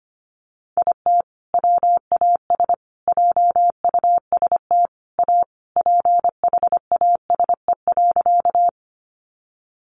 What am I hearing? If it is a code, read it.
Morse code, heard as a keyed tone, about 25 wpm, 700 Hz. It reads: IT WAS JUST A PHASE.
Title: Sentences from Top 1200 Words 25wpm